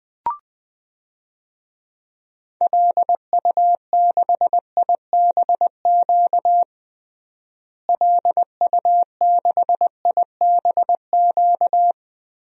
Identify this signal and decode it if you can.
Morse code, heard as a keyed tone, about 21 wpm, 700 Hz. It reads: LU6IBQ LU6IBQ
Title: DXCC-100 Call Signs - 6 Characters - Repeated 2x - 20wpm